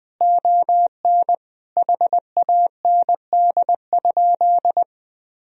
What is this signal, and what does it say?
Morse code, heard as a keyed tone, about 20 wpm, 700 Hz. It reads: ON HAND?
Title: Sentences from Top 1000 Words - Review Part 4 - 20wpm